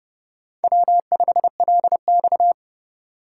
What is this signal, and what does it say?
Morse code, heard as a keyed tone, about 29 wpm, 700 Hz. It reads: W5LX